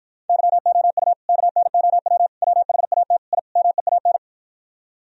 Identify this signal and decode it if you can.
Morse code, heard as a keyed tone, about 52 wpm, 700 Hz. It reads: YOU KNOW WHAT I MEAN